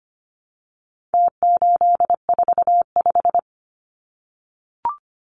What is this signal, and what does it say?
Morse code, heard as a keyed tone, about 25 wpm, 700 Hz. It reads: T845
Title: Numbers - Contest Style - 4-Digit 25wpm